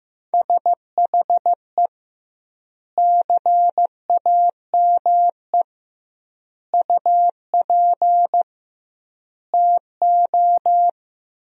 Morse code, heard as a keyed tone, about 15 wpm, 700 Hz. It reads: SHE CAME UP TO
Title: 2x Word-Spacing - Sentences from Top 200 Words - 15wpm